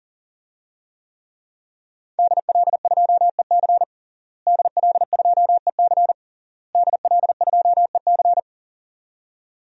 Morse code, heard as a keyed tone, about 40 wpm, 700 Hz. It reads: DL2EC DL2EC DL2EC
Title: DXCC-100 Call Signs - 5 Characters - Repeated 3x - 40wpm